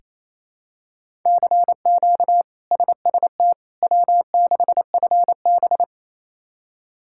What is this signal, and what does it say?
Morse code, heard as a keyed tone, about 28 wpm, 700 Hz. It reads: CQ SST W6FB
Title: K1USN SST QSO Practice - 28wpm